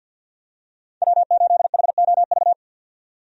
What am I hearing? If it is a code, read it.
Morse code, heard as a keyed tone, about 45 wpm, 700 Hz. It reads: W8HOV
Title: USA Call Signs - 5 Characters - 50wpm